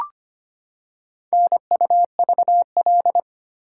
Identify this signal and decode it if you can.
Morse code, heard as a keyed tone, about 25 wpm, 700 Hz. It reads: NUVL
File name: practice / 25wpm / Four Letter-Number Combinations - 25wpm